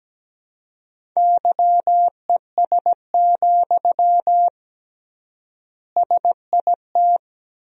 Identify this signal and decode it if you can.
Morse code, heard as a keyed tone, about 17 wpm, 700 Hz. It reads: YES, SIT
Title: 3x Word-Spacing - Sentences from Top 500 Words - 17wpm